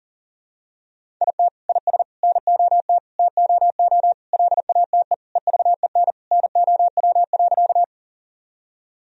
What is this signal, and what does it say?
Morse code, heard as a keyed tone, about 40 wpm, 700 Hz. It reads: IT IS NOT TOO LATE EVEN NOW.
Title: Sentences from Top 1000 Words - Review Part 1 - 40wpm